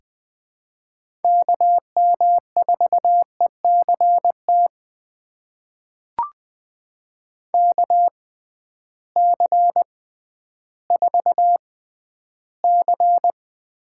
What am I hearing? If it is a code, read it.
Morse code, heard as a keyed tone, about 20 wpm, spaced as for 12 wpm, 700 Hz. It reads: KM4ECT KC4C
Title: USA Call Signs - 6 Characters - 6x Character Spacing - 20wpm